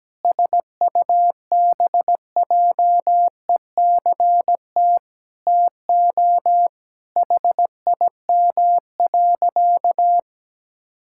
Morse code, heard as a keyed tone, about 17 wpm, 700 Hz. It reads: SUBJECT TO HIM.
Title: Sentences from Top 1600 Words 17wpm